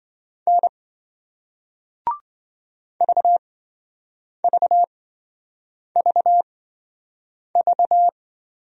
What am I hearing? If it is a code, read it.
Morse code, heard as a keyed tone, about 28 wpm, spaced as for 7 wpm, 700 Hz. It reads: N VVVV